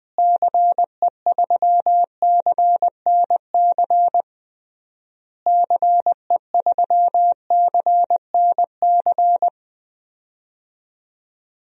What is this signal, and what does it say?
Morse code, heard as a keyed tone, about 20 wpm, 700 Hz. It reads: CE3CNC CE3CNC
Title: DXCC-100 Call Signs - 6 Characters - Repeated 3x - 20wpm